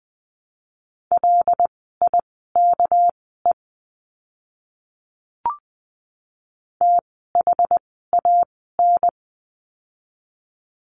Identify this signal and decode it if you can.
Morse code, heard as a keyed tone, about 20 wpm, spaced as for 15 wpm, 700 Hz. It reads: LIKE THAN
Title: 244 Lesson 39f - Letter z - Words 20wpm